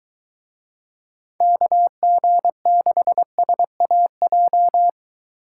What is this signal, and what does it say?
Morse code, heard as a keyed tone, about 23 wpm, 700 Hz. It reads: KG6SAJ